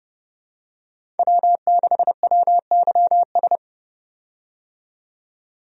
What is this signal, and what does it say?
Morse code, heard as a keyed tone, about 30 wpm, 700 Hz. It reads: W6WYS
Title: USA Call Signs - 5 Characters - 30wpm